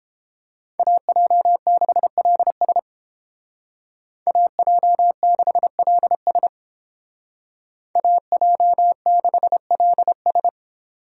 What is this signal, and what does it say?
Morse code, heard as a keyed tone, about 30 wpm, 700 Hz. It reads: AJ6LS AJ6LS AJ6LS